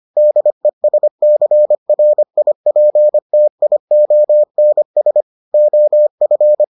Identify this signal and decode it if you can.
Morse code, heard as a keyed tone, about 25 wpm, 600 Hz. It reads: DESCRIPTIONS OF